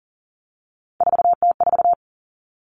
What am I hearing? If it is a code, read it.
Morse code, heard as a keyed tone, about 40 wpm, 700 Hz. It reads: 4T4
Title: Common Cut Numbers - 3-Digit 40wpm